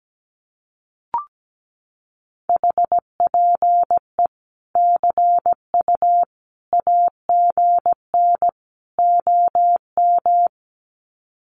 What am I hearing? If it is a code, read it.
Morse code, heard as a keyed tone, about 17 wpm, 700 Hz. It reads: HPE CU AGN OM
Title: Common QSO Phrases 17wpm